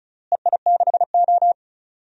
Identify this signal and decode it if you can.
Morse code, heard as a keyed tone, about 36 wpm, 700 Hz. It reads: EI6O
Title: DXCC-100 Call Signs - 4 Characters - Repeated 2x - 35wpm